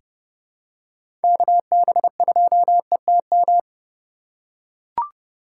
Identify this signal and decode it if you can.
Morse code, heard as a keyed tone, about 30 wpm, 700 Hz. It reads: KB2ETM
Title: USA Call Signs - 6 Characters - 3x Character Spacing - 30wpm